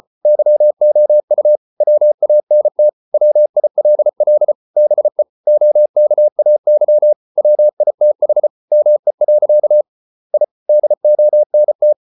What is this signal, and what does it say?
Morse code, heard as a keyed tone, about 34 wpm, 600 Hz. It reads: YOU WANT WILL BE OKAY WITH ME. I DONT